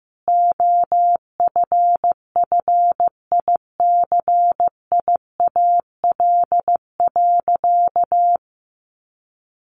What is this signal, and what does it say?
Morse code, heard as a keyed tone, about 15 wpm, 700 Hz. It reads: OFFICIAL.
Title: Sentences from Top 1000 Words 15wpm